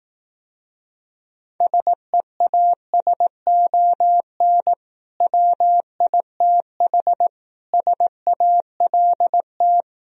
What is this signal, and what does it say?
Morse code, heard as a keyed tone, about 18 wpm, 700 Hz. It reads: SEASON WITH SALT
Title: Speed-Racing - Sets of 5 Words 15wpm